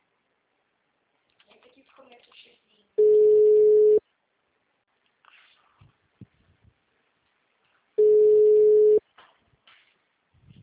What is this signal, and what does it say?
Ringback tone (Europe, 425 Hz), with speech.
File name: out- -2025- - - .325